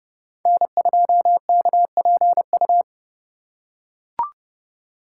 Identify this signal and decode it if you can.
Morse code, heard as a keyed tone, about 31 wpm, 700 Hz. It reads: N2KPU
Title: USA Call Signs - 3x Character Spacing - 30wpm